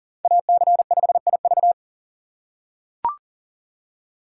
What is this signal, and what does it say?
Morse code, heard as a keyed tone, about 40 wpm, 700 Hz. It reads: AC5IV
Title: USA Call Signs - 3x Character Spacing - 40wpm